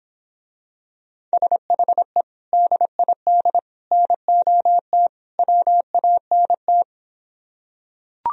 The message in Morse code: SHE DID NOT WANT

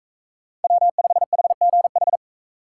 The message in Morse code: W5HGH

